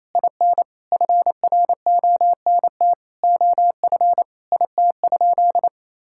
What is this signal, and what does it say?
Morse code, heard as a keyed tone, about 27 wpm, 700 Hz. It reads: IN FRONT OF IT?
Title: Sentences from Top 1000 Words - Review Part 5 - 28wpm